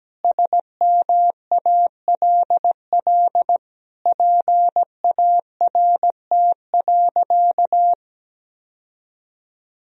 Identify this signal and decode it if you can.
Morse code, heard as a keyed tone, about 17 wpm, 700 Hz. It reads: SMALL PART.
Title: Sentences from Top 1000 Words - Review Part 1 - 17wpm